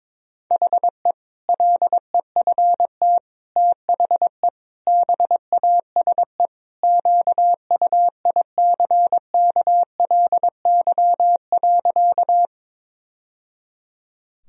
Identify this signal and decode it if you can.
Morse code, heard as a keyed tone, about 22 wpm, 700 Hz. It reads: HE LEFT THE BASE QUICKLY.
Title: Sentences from Top 1000 Words - Review Part 1 - 22wpm